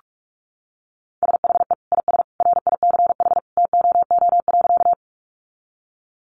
Morse code, heard as a keyed tone, about 46 wpm, 700 Hz. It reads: SHE IS RICH TOO.
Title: Sentences from Top 1100 Words 45wpm